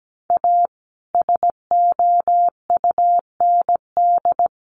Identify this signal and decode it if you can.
Morse code, heard as a keyed tone, about 17 wpm, 700 Hz. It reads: A SOUND
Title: A Sound Of Thunder 17wpm